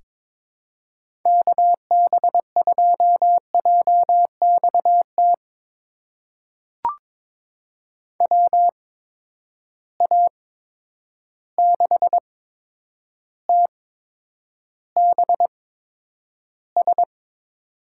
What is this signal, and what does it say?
Morse code, heard as a keyed tone, about 22 wpm, spaced as for 9 wpm, 700 Hz. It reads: KB2JXT WA6TBS
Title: USA Call Signs - 6 Characters - 8x Character Spacing - 22wpm